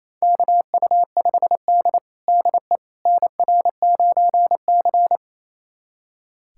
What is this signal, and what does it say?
Morse code, heard as a keyed tone, about 29 wpm, 690 Hz. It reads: KU5D DE NR9C
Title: USA Callsign DE Callsign - 4 Characters - 28wpm